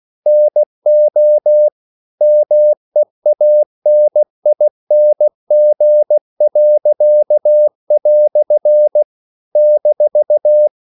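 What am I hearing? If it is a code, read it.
Morse code, heard as a keyed tone, about 16 wpm, 600 Hz. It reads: NO MEANING." -
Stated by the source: English quotations sent in CW as practice at 16 words per minute.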